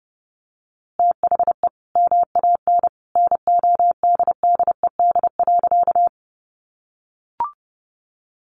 Morse code, heard as a keyed tone, about 30 wpm, 700 Hz. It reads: THE MAN NODDED.
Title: A Sound Of Thunder 30wpm